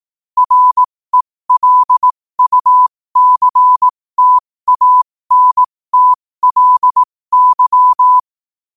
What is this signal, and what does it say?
Morse code, heard as a keyed tone, about 18 wpm, spaced as for 15 wpm, 1000 Hz. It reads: RELUCTANTLY
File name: qotdfiles / 15wpm